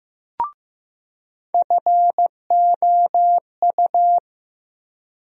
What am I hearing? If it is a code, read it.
Morse code, heard as a keyed tone, about 15 wpm, 700 Hz. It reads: FOU
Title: USA Call Sign Postfixes - 3 Letters 15wpm